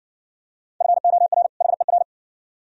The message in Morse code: FOR HER

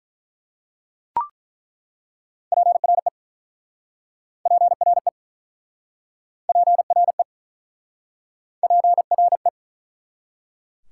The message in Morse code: PRE PRE PRE PRE

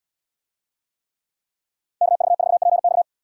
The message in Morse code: KV4PX

